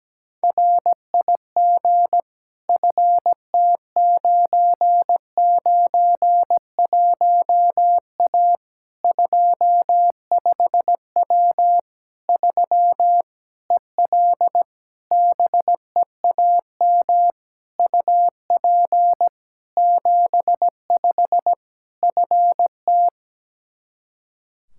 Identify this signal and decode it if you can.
Morse code, heard as a keyed tone, about 17 wpm, 700 Hz. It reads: RIG FT991A 25W 3 EL BEAM UP 75 FT
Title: K1USN SST QSO Practice 17wpm